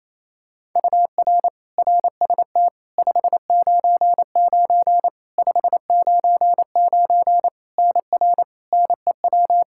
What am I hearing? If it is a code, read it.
Morse code, heard as a keyed tone, about 28 wpm, 700 Hz. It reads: UR RST 599 599 NR NEW
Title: Practice QSO - Exchange 1 - Beginning - 28wpm